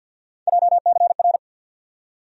Morse code, heard as a keyed tone, about 50 wpm, 700 Hz. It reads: JCR
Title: USA Call Sign Postfixes - 3 Letters 50wpm